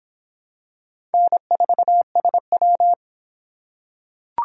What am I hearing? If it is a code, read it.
Morse code, heard as a keyed tone, about 23 wpm, 700 Hz. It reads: N4SW